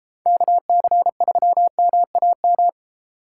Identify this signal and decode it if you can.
Morse code, heard as a keyed tone, about 26 wpm, 700 Hz. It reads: KC3MAM
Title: Speed Racing - USA 2x3 Call Signs 22wpm